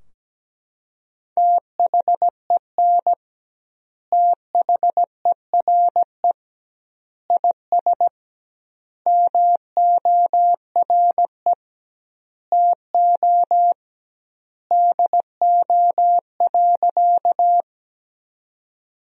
Morse code, heard as a keyed tone, about 17 wpm, 700 Hz. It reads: THEN THERE IS MORE TO DO.
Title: 2x Word-Spacing - Sentences from Top 100 Words - 17wpm